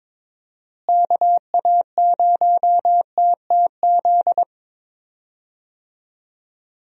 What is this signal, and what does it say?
Morse code, heard as a keyed tone, about 22 wpm, 700 Hz. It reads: KA0TTZ
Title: USA Call Signs - 6 Characters - Repeated 2x - 22wpm